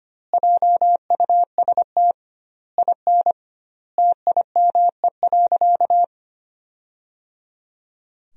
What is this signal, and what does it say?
Morse code, heard as a keyed tone, about 25 wpm, 700 Hz. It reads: JUST IN TIME.
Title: 2x Word-Spacing - Sentences from Top 300 Words - 25wpm